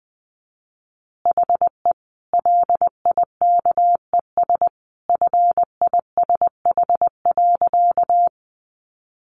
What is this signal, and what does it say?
Morse code, heard as a keyed tone, about 19 wpm, 700 Hz. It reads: HE LIKES FISH.
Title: Easy Phrases 20wpm